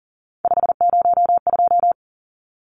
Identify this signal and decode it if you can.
Morse code, heard as a keyed tone, about 40 wpm, 700 Hz. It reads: 502
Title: Numbers - 3-Digit 40wpm